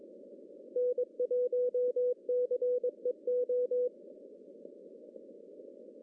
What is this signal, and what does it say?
Morse code, heard as a keyed tone, about 22 wpm, 500 Hz. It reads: N1CEO
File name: CW / cqww-cw-k5tr